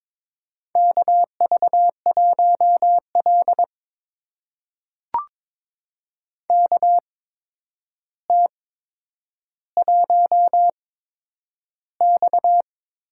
Morse code, heard as a keyed tone, about 22 wpm, spaced as for 9 wpm, 700 Hz. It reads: KV1L KT1X
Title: USA Call Signs - 4 Characters - 8x Character Spacing - 22wpm